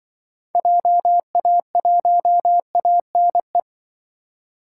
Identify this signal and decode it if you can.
Morse code, heard as a keyed tone, about 25 wpm, 700 Hz. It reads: JA1ANE